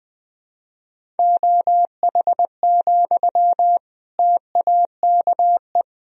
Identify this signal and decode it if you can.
Morse code, heard as a keyed tone, about 20 wpm, 700 Hz. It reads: OH, TAKE
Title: Sentences from Top 1000 Words - Review Part 1 - 20wpm